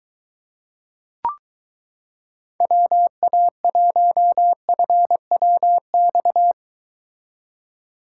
Morse code, heard as a keyed tone, about 23 wpm, 700 Hz. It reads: WA1FWX